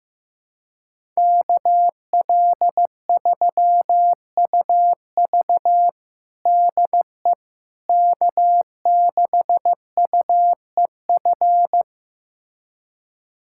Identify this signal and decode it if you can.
Morse code, heard as a keyed tone, about 15 wpm, 700 Hz. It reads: KL3UV DE K6UEF
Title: USA Callsign DE Callsign - 5 Characters - 15wpm